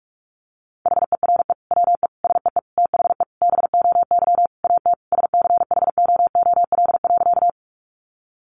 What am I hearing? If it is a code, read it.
Morse code, heard as a keyed tone, about 44 wpm, 700 Hz. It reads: HERE WE SEE THE BOY AT SCHOOL.
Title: Sentences from Top 300 Words 45wpm